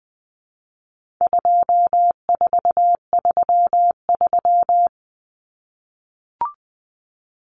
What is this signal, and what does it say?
Morse code, heard as a keyed tone, about 19 wpm, 700 Hz. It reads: 2433
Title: Common Cut Numbers - 4-Digit 20wpm